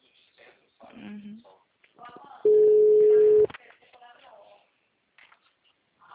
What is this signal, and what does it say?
Ringback tone (Europe, 425 Hz), with speech.